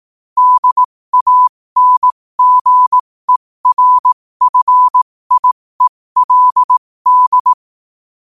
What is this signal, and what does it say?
Morse code, heard as a keyed tone, about 18 wpm, spaced as for 15 wpm, 1000 Hz. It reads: DANGERFIELD
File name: 15wpm